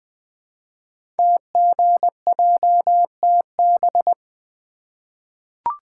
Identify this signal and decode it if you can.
Morse code, heard as a keyed tone, about 20 wpm, 700 Hz. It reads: TGJTB
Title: Code Groups - 5 Letters - 20wpm